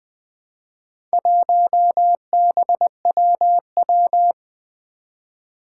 Morse code, heard as a keyed tone, about 20 wpm, 700 Hz. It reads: 1BWW
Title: Four Challenging Letter-Number Combinations 20wpm